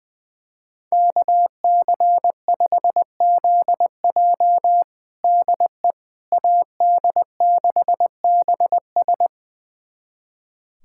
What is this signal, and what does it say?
Morse code, heard as a keyed tone, about 21 wpm, 700 Hz. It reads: KC5ZJ DE AD6BS
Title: USA Callsign DE Callsign - 5 Characters - 20wpm